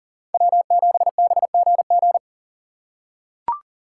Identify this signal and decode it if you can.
Morse code, heard as a keyed tone, about 39 wpm, 690 Hz. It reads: W7BGG
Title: USA Call Signs - 2x Character Spacing - 40wpm